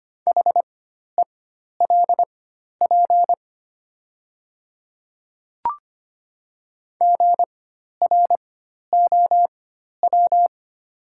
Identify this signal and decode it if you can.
Morse code, heard as a keyed tone, about 25 wpm, spaced as for 12 wpm, 700 Hz. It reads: HELP GROW
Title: 4x Character Spacing - Top 300 Words - 25wpm